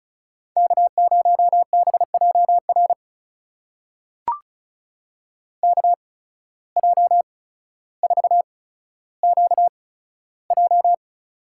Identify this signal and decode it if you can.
Morse code, heard as a keyed tone, about 36 wpm, spaced as for 15 wpm, 700 Hz. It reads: K0BJR KJ4QJ
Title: USA Call Signs - 5 Characters - 8x Character Spacing - 35wpm